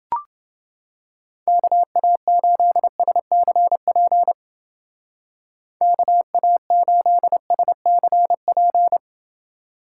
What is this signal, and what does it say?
Morse code, heard as a keyed tone, about 29 wpm, 700 Hz. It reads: KA8SCP KA8SCP